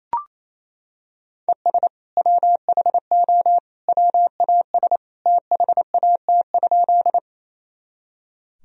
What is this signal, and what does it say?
Morse code, heard as a keyed tone, about 28 wpm, 700 Hz. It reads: ES WHO WAS THAT?
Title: Sentences from Top 100 Words and Abbreviations 28wpm